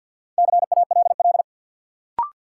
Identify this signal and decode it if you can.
Morse code, heard as a keyed tone, about 50 wpm, 700 Hz. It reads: CALL
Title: Sentences from Top 100 Words - New Word Drill - 50wpm